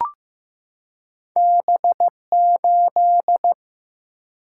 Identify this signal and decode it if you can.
Morse code, heard as a keyed tone, about 15 wpm, 700 Hz. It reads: B8
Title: Single-Letter Single-Number 15wpm